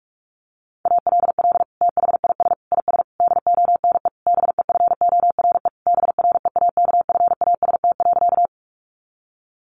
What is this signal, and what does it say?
Morse code, heard as a keyed tone, about 44 wpm, 700 Hz. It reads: ALL THIS IS DONE BEFORE BREAKFAST.
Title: Sentences from Top 1500 Words 45wpm